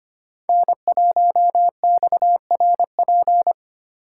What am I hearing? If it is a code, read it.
Morse code, heard as a keyed tone, about 29 wpm, 700 Hz. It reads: N1XRP
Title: Speed Racing - USA 1x3 Call Signs 25wpm